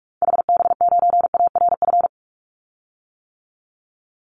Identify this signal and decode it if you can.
Morse code, heard as a keyed tone, about 44 wpm, 700 Hz. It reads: HB9ARF
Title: Call Signs - 45wpm